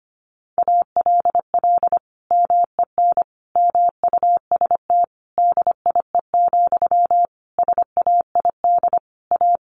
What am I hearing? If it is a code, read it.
Morse code, heard as a keyed tone, about 25 wpm, 700 Hz. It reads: ALL MEN MUST DIE, SAID A
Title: Sentences from Top 1300 Words 25wpm